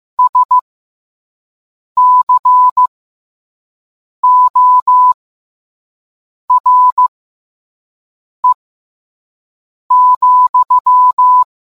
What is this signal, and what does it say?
Morse code, heard as a keyed tone, about 15 wpm, spaced as for 5 wpm, 1000 Hz. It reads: SCORE,